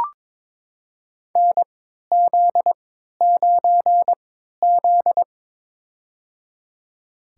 Morse code, heard as a keyed tone, about 22 wpm, spaced as for 12 wpm, 700 Hz. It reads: NZ9Z
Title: USA Call Signs - 4 Characters - 3x Character Spacing - 22wpm